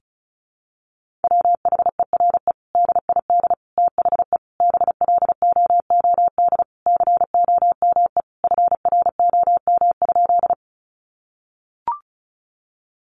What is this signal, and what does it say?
Morse code, heard as a keyed tone, about 34 wpm, 700 Hz. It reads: WHERE DID THE BLOOD COME FROM?
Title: Sentences from Top 1100 Words 35wpm